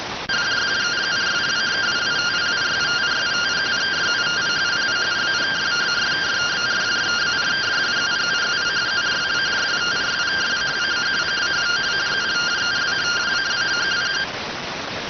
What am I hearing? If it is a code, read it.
Two-tone teleprinter signal, about 170 Hz shift.